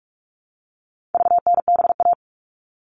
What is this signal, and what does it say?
Morse code, heard as a keyed tone, about 44 wpm, 700 Hz. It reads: VNBA
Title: All Cut Numbers - 4-Digit 45wpm